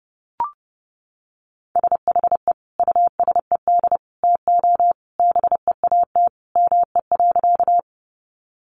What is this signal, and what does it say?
Morse code, heard as a keyed tone, about 30 wpm, 700 Hz. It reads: SHE USED TO BEAT ME.
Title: Sentences from Top 1300 Words 30wpm